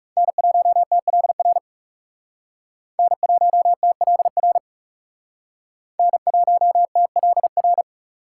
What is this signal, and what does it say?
Morse code, heard as a keyed tone, about 38 wpm, 700 Hz. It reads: N1TLR N1TLR N1TLR